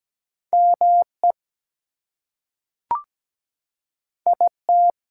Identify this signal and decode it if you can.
Morse code, heard as a keyed tone, about 17 wpm, 700 Hz. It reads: ME IT
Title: Two Letter Words - 17wpm